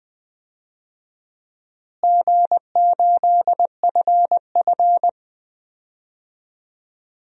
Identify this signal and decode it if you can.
Morse code, heard as a keyed tone, about 20 wpm, 700 Hz. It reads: G8FF
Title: DXCC-100 Call Signs - 4 Characters - Repeated 2x - 20wpm